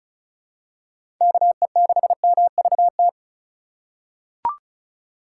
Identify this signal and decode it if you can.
Morse code, heard as a keyed tone, about 36 wpm, 700 Hz. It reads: KE6MVT
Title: USA Call Signs - 6 Characters - 3x Character Spacing - 35wpm